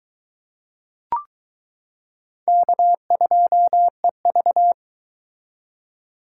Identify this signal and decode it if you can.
Morse code, heard as a keyed tone, about 22 wpm, 700 Hz. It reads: K2EV